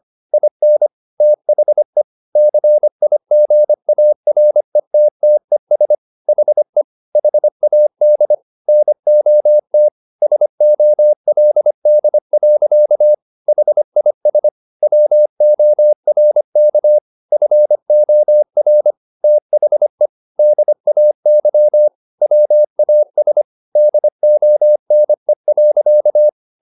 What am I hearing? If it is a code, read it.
Morse code, heard as a keyed tone, about 25 wpm, 600 Hz. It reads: IN THE CIGARETTES HE HAD NOT SOLD. HIS WORK FOR THE DAY WAS DONE.